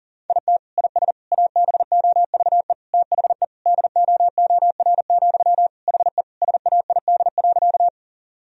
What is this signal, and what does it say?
Morse code, heard as a keyed tone, about 40 wpm, 700 Hz. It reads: IT IS ABOVE THE DOOR, HE SAID.
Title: Sentences from Top 1000 Words - Review Part 2 - 40wpm